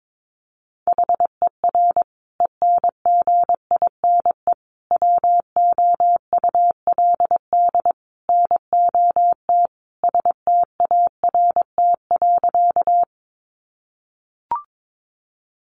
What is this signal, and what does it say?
Morse code, heard as a keyed tone, about 22 wpm, 700 Hz. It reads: HER ENGINE WOULD NOT START.
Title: Sentences from Top 1500 Words 22wpm